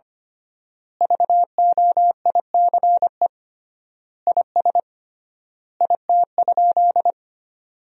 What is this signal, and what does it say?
Morse code, heard as a keyed tone, about 25 wpm, 700 Hz. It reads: VOICE IS IT?
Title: 3x Word-Spacing - Sentences from Top 500 Words - 25wpm